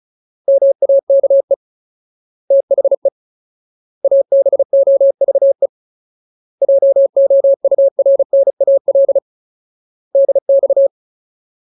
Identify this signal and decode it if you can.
Morse code, heard as a keyed tone, about 34 wpm, 550 Hz. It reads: MAKE THE ABOVE JOURNAL DX
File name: Ultimate Word List / 35WPM Ultimate